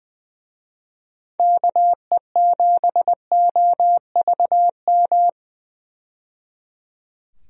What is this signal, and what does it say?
Morse code, heard as a keyed tone, about 20 wpm, 700 Hz. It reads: KE7OVM
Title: USA Call Signs - 6 Characters - 20wpm